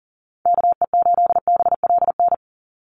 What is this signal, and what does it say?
Morse code, heard as a keyed tone, about 40 wpm, 700 Hz. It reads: KE8BLN
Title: USA Call Signs 40wpm